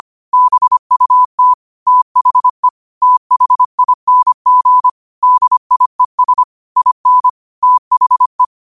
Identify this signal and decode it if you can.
Morse code, heard as a keyed tone, about 25 wpm, 1000 Hz. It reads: BUT THE THING DIES IN THE
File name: qotdfiles / 25wpm